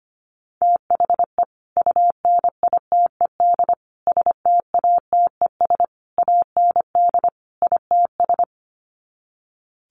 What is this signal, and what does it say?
Morse code, heard as a keyed tone, about 25 wpm, 700 Hz. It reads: THE UNITED STATES AND ITS
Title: Sets of 5 Words Encore 25wpm